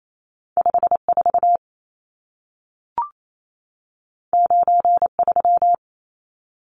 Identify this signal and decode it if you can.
Morse code, heard as a keyed tone, about 28 wpm, 700 Hz. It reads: 54 93
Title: Numbers - 2-Digit 28wpm